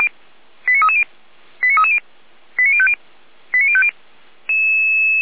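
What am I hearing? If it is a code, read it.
Five-tone selective call (ZVEI), heard as quick runs of five tones, tones of about 70 ms. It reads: 89330 89330 89050 89050
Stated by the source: Five-tone ZVEI selective call 89319, received by SDR, from feuerwehr-orlamuende.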